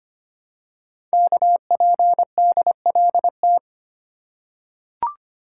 Morse code, heard as a keyed tone, about 25 wpm, 700 Hz. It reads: KPDLT
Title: Code Groups - 5 Letters - 25wpm